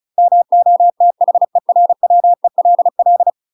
Morse code, heard as a keyed tone, about 36 wpm, 700 Hz. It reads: MOTHERWELL
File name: UK MASTER QSOs TRAINING / 35WPM / UK MASTER QSOs 0035